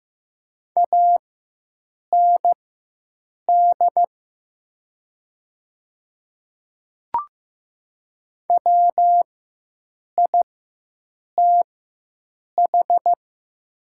Morse code, heard as a keyed tone, about 15 wpm, spaced as for 7 wpm, 700 Hz. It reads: AND WITH
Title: 4x Character Spacing - Top 200 Words - 15wpm